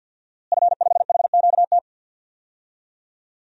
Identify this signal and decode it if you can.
Morse code, heard as a keyed tone, about 47 wpm, 700 Hz. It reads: F5HQT
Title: DXCC-100 Call Signs - 5 Characters - Repeated 2x - 50wpm